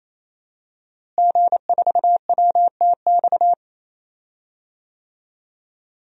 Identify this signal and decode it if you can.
Morse code, heard as a keyed tone, about 28 wpm, 700 Hz. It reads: G4WTX
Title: DXCC-100 Call Signs - 5 Characters - Repeated 3x - 28wpm